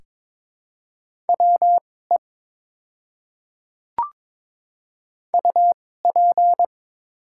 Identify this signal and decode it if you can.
Morse code, heard as a keyed tone, about 22 wpm, spaced as for 16 wpm, 700 Hz. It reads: WE UP
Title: 2x Character Spacing - Top 100 Words - 22wpm